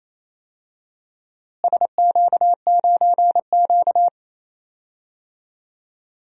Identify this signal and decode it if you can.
Morse code, heard as a keyed tone, about 28 wpm, 700 Hz. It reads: SQ9Q
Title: DXCC-100 Call Signs - 4 Characters - Repeated 3x - 28wpm